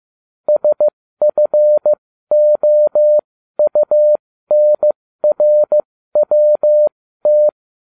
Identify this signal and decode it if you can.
Morse code, heard as a keyed tone, about 15 wpm, spaced as for 13 wpm, 600 Hz. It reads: SFOUNRWT